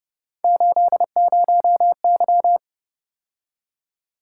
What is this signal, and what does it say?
Morse code, heard as a keyed tone, about 29 wpm, 700 Hz. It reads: 80Y